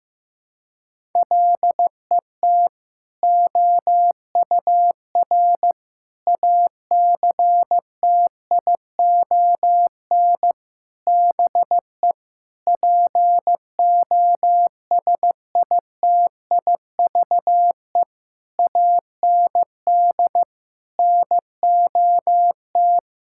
Morse code, heard as a keyed tone, about 15 wpm, 700 Hz. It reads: LET OUR ACTION BE POSITIVE AND NOT
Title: Sentences from Top 1000 Words - Review Part 6 - 15wpm